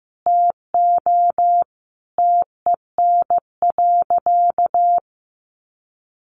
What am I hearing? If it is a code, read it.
Morse code, heard as a keyed tone, about 15 wpm, 700 Hz. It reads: TO TEN.
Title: Sentences from Top 1300 Words 15wpm